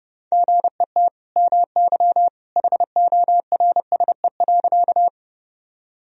Morse code, heard as a keyed tone, about 32 wpm, 700 Hz. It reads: GET MY HORSE.
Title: Sentences from Top 1000 Words - Review Part 1 - 30wpm